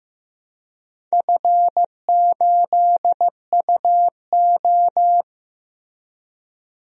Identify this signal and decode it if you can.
Morse code, heard as a keyed tone, about 15 wpm, 700 Hz. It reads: F8UO
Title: DXCC-100 Call Signs - 4 Characters - Repeated 2x - 15wpm